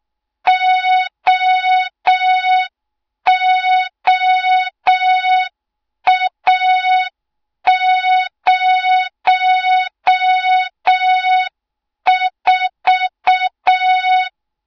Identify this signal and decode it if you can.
Morse code, heard as a keyed tone, about 6 wpm, 3700 Hz. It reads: OOA04
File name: Morze-HA2MN / hang / 20 GYAKORLAT